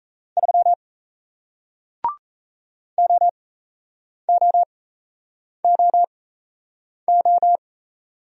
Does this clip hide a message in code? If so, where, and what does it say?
Yes: Morse code from 0.4 to 7.6 s, heard as a keyed tone, about 37 wpm, spaced as for 8 wpm, 700 Hz. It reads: 3 OOOO